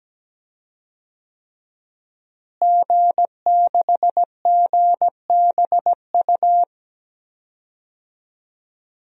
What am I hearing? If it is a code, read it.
Morse code, heard as a keyed tone, about 17 wpm, 700 Hz. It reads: G6GBU
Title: DXCC-100 Call Signs - 5 Characters - Repeated 3x - 17wpm